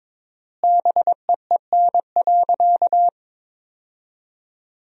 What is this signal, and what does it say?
Morse code, heard as a keyed tone, about 22 wpm, 700 Hz. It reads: BEEN.
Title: 4x Word-Spacing - Sentences from Top 300 Words - 22wpm